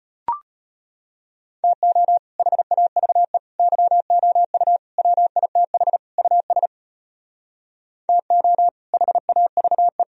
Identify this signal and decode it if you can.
Morse code, heard as a keyed tone, about 33 wpm, 700 Hz. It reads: TO HAVE YOU WITH US TO HAVE